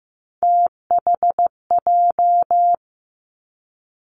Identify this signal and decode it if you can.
Morse code, heard as a keyed tone, about 15 wpm, 700 Hz. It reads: THJ
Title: Three Challenging Letter-Number Combinations 15wpm